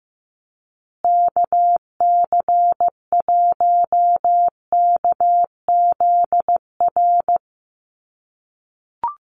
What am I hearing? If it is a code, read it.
Morse code, heard as a keyed tone, about 15 wpm, 700 Hz. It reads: KC1KZR
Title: USA Call Signs Repeated 2x 15wpm